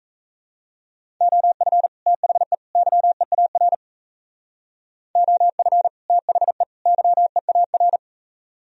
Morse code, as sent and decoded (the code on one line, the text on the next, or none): --- ..-. / - .... / -.-- . .- .-. / --- ..-. / - .... / -.-- . .- .-.
OF THE YEAR OF THE YEAR